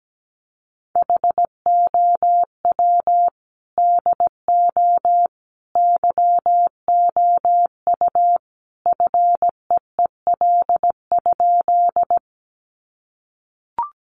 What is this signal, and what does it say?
Morse code, heard as a keyed tone, about 17 wpm, 700 Hz. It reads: HOW DO YOU FEEL?
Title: Sentences from Top 400 Words 17wpm